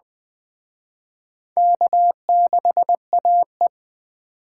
Morse code, heard as a keyed tone, about 21 wpm, 700 Hz. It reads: K6AE
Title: Speed Racing - USA 1x2 Call Signs 17wpm